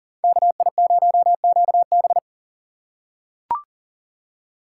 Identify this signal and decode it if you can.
Morse code, heard as a keyed tone, about 40 wpm, 700 Hz. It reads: KI0QB
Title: USA Call Signs - 3x Character Spacing - 40wpm